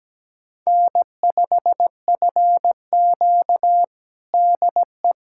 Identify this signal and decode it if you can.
Morse code, heard as a keyed tone, about 17 wpm, 700 Hz. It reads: N5FQ DE
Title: USA Callsign DE Callsign - 4 Characters - 17wpm